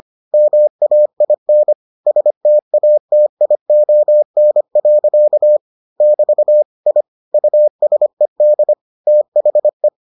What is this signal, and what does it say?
Morse code, heard as a keyed tone, about 25 wpm, 600 Hz. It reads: MAIN STATION. <BT> I USED THE